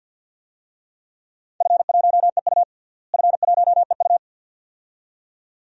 Morse code, heard as a keyed tone, about 46 wpm, 700 Hz. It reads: F1EU F1EU
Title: DXCC-100 Call Signs - 4 Characters - Repeated 2x - 50wpm